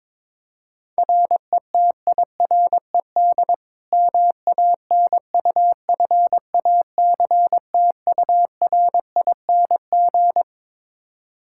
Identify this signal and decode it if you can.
Morse code, heard as a keyed tone, about 22 wpm, 700 Hz. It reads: RETIRED MANUFACTURING
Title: K1USN SST QSO Practice 22wpm